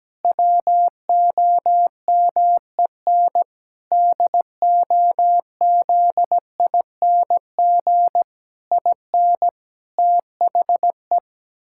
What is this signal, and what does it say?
Morse code, heard as a keyed tone, about 17 wpm, 700 Hz. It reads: WOMEN DOZING IN THE
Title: Pangram Challenge v2 - 17wpm